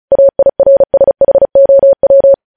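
Morse code, heard as a keyed tone, about 36 wpm, 550 Hz. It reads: AIRSHOW
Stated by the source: Morse practice sending news headlines at 35 wpm.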